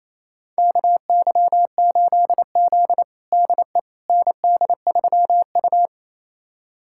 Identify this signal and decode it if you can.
Morse code, heard as a keyed tone, about 28 wpm, 700 Hz. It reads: KY8Z DE ND3U
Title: USA Callsign DE Callsign - 4 Characters - 28wpm